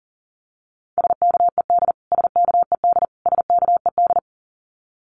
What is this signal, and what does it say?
Morse code, heard as a keyed tone, about 40 wpm, 700 Hz. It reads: SKED SKED SKED
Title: Abbreviations Repeated 3x - 40wpm